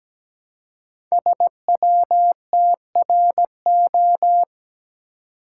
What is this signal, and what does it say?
Morse code, heard as a keyed tone, about 17 wpm, 700 Hz. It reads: SWTRO
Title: Code Groups - 5 Letters - 17wpm